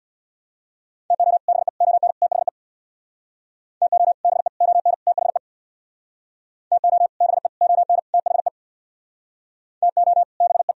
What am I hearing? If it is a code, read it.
Morse code, heard as a keyed tone, about 62 wpm, 700 Hz. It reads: TO BE ON THE TO BE ON THE TO BE ON THE TO BE
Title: Speed-Racing - Sets of 4 Words 50wpm